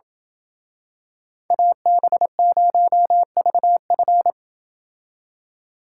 Morse code, heard as a keyed tone, about 27 wpm, 700 Hz. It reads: AB0VF